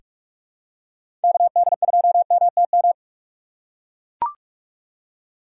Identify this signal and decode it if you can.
Morse code, heard as a keyed tone, about 44 wpm, 700 Hz. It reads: KD2MTM
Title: USA Call Signs - 6 Characters - 6x Character Spacing - 45wpm